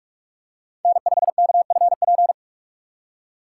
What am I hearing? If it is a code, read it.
Morse code, heard as a keyed tone, about 41 wpm, 700 Hz. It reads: N5KFP
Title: USA Call Signs - 5 Characters - 2x Character Spacing - 45wpm